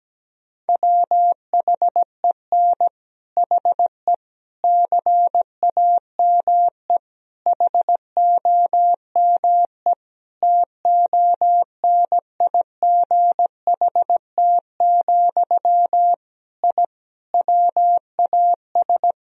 Morse code, heard as a keyed tone, about 17 wpm, 700 Hz. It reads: WHEN HE CAME HOME TONIGHT, I WAS
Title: Sentences from Top 1000 Words - Review Part 3 - 17wpm